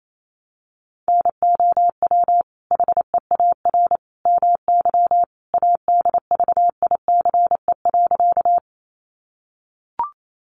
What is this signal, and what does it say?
Morse code, heard as a keyed tone, about 28 wpm, 700 Hz. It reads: NOW HEAR MY ADVICE.
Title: Sentences from Top 1100 Words 28wpm